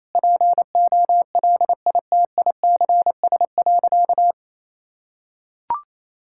Morse code, heard as a keyed tone, about 28 wpm, 700 Hz. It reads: POLITICS.
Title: Sentences from Top 1000 Words - Review Part 6 - 28wpm